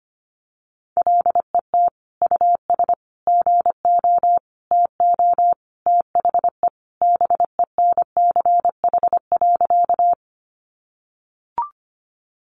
Morse code, heard as a keyed tone, about 25 wpm, 700 Hz. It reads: LET US GO TO THE BENCH.
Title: Sentences from Top 1400 Words 25wpm